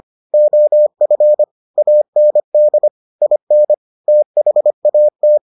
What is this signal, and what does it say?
Morse code, heard as a keyed tone, about 25 wpm, 600 Hz. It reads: OF AND IN THAT